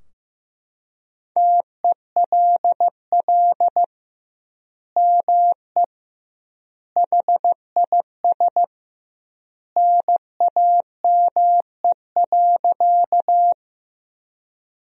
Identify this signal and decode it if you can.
Morse code, heard as a keyed tone, about 15 wpm, 700 Hz. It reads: TELL ME HIS NAME.
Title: 2x Word-Spacing - Sentences from Top 200 Words - 15wpm